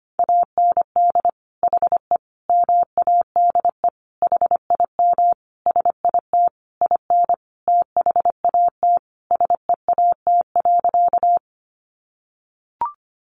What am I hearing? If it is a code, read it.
Morse code, heard as a keyed tone, about 25 wpm, 700 Hz. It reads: AND HE MADE HIM SIT IN THAT SEAT.
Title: Sentences from Top 1500 Words 25wpm